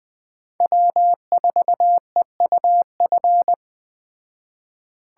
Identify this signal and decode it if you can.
Morse code, heard as a keyed tone, about 20 wpm, 700 Hz. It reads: W4EUF
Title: Speed Racing - USA 1x3 Call Signs 17wpm